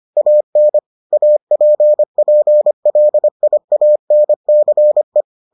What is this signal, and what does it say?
Morse code, heard as a keyed tone, about 25 wpm, 600 Hz. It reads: AN APPLIANCE